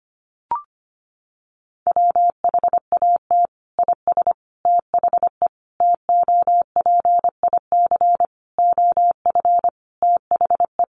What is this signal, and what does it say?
Morse code, heard as a keyed tone, about 25 wpm, 700 Hz. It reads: WHAT IS THE TOPIC OF THE
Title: Sentences from Top 1200 Words 25wpm